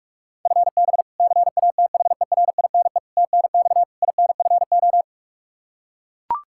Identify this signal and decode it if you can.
Morse code, heard as a keyed tone, about 41 wpm, 700 Hz. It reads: FB CATHERINE TNX INFO
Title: K1USN SST QSO Practice 45wpm